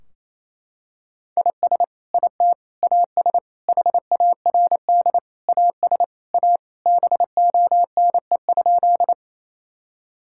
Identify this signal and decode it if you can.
Morse code, heard as a keyed tone, about 28 wpm, 700 Hz. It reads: IS IT AS HARD AS A BONE?
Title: Sentences from Top 1600 Words 28wpm